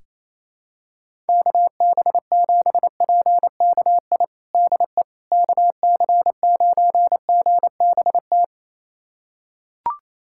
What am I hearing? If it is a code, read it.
Morse code, heard as a keyed tone, about 27 wpm, 700 Hz. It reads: KB7PKI DE KC9GBT
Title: USA Callsign DE Callsign - 6 Characters - 28wpm